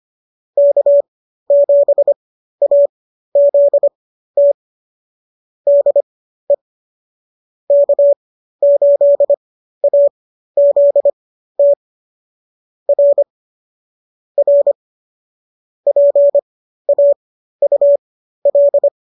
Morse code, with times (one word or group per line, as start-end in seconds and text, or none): 0.6-4.5 K7AZT
5.7-6.5 DE
7.7-11.7 K8AZT
12.9-13.2 R
14.4-14.7 R
15.9-18.9 PAUL